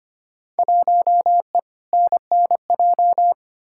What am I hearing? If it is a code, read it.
Morse code, heard as a keyed tone, about 25 wpm, 700 Hz. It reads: 1E NNJ
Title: Field Day Exchanges - 25wpm